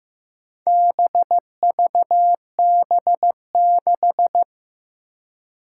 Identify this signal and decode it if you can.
Morse code, heard as a keyed tone, about 15 wpm, 700 Hz. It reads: BVB6
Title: Four Challenging Letter-Number Combinations 15wpm